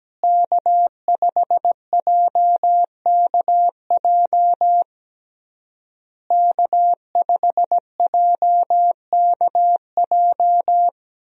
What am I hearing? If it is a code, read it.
Morse code, heard as a keyed tone, about 17 wpm, 700 Hz. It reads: K5JKJ K5JKJ
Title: USA Call Signs - 5 Characters - Repeated 3x - 17wpm